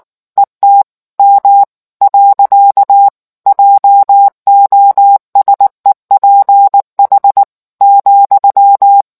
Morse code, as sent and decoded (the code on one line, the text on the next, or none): . - / -- / .-.-.- / .--- --- ... .--. .... / --..--
ET M . JOSEPH ,